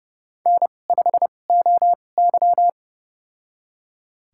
Morse code, heard as a keyed tone, about 29 wpm, spaced as for 21 wpm, 700 Hz. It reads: N5OY
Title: USA Call Signs - 2x Character Spacing - 30wpm